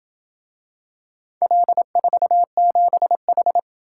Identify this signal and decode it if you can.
Morse code, heard as a keyed tone, about 28 wpm, 700 Hz. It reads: L47H